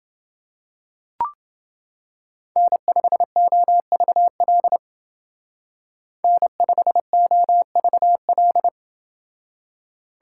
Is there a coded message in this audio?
Yes: Morse code from 2.6 to 8.7 s, heard as a keyed tone, about 29 wpm, 700 Hz. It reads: N5OVL N5OVL